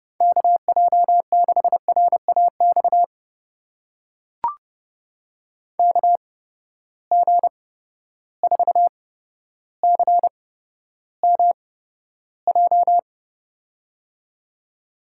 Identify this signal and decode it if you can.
Morse code, heard as a keyed tone, about 30 wpm, spaced as for 13 wpm, 700 Hz. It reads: KJ6RAX KG4CMJ
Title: USA Call Signs - 6 Characters - 8x Character Spacing - 30wpm